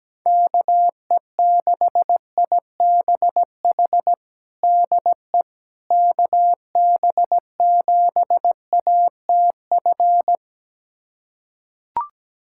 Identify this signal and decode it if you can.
Morse code, heard as a keyed tone, about 17 wpm, 700 Hz. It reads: KE6IBH DE KB7ATF
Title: USA Callsign DE Callsign - 6 Characters - 17wpm